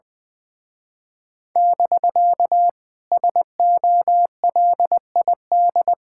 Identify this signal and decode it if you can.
Morse code, heard as a keyed tone, about 20 wpm, 700 Hz. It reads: <BK> SOLID
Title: Speed-Racing Common QSO Phrases Encore - 20wpm